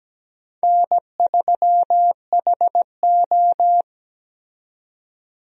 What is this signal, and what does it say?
Morse code, heard as a keyed tone, about 17 wpm, 700 Hz. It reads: N3HO
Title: USA Call Signs - 4 Characters - 17wpm